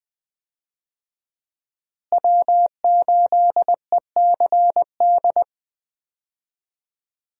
Morse code, heard as a keyed tone, about 20 wpm, 700 Hz. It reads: W8ECD
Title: USA Call Signs - 5 Characters - Repeated 3x - 20wpm